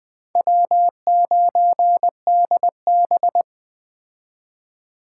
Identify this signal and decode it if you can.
Morse code, heard as a keyed tone, about 22 wpm, 700 Hz. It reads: W9DB